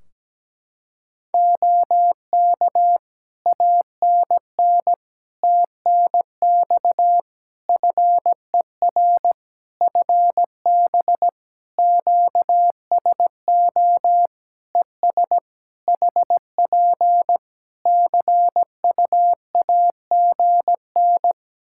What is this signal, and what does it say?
Morse code, heard as a keyed tone, about 17 wpm, 700 Hz. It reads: OK ANN TNX FER FB QSO ES HP CUAGN
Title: Practice QSO - Exchange 4 - Ending - 17wpm